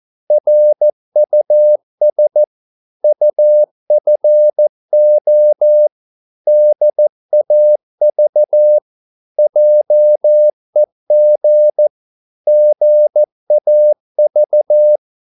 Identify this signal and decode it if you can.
Morse code, heard as a keyed tone, about 14 wpm, 600 Hz. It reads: RUS UFO DAV JEG GAV
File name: Korte ord DK 14wpm